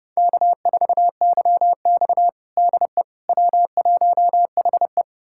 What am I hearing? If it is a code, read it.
Morse code, heard as a keyed tone, about 30 wpm, 700 Hz. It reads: K4YX DE W1HE
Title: USA Callsign DE Callsign - 4 Characters - 30wpm